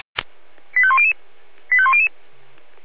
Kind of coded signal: five-tone selective call